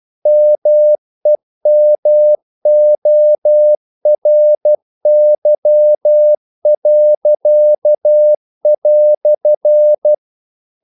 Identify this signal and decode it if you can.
Morse code, heard as a keyed tone, about 12 wpm, 600 Hz. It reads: MEMORY."
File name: CW / Practice / EnglishQuotations0312